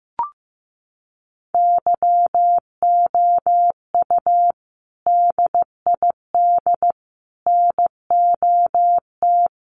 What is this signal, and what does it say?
Morse code, heard as a keyed tone, about 15 wpm, 700 Hz. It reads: YOU DID NOT
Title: Sentences from Top 1100 Words 15wpm